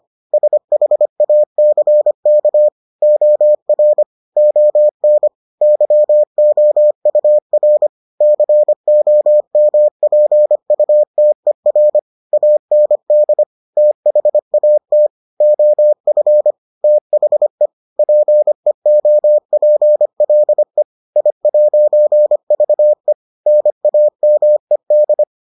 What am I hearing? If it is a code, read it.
Morse code, heard as a keyed tone, about 25 wpm, 600 Hz. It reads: SHACK OR ON YOUR COMPUTER AND THAT OF THE PEOPLE I'VE NAMED